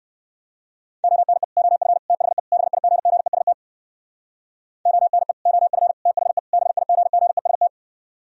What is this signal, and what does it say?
Morse code, heard as a keyed tone, about 60 wpm, 700 Hz. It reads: ONE OF THE BIGGEST ONE OF THE BIGGEST